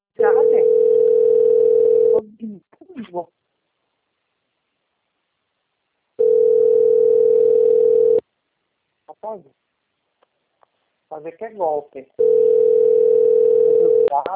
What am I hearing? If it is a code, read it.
Ringback tone (North America), 440 + 480 Hz, with speech.
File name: out- - - - - .4587